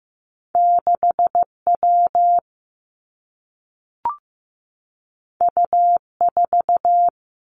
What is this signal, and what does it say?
Morse code, heard as a keyed tone, about 15 wpm, 700 Hz. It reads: 6 W U 4
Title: Two Challenging Letter-Number Combinations 15wpm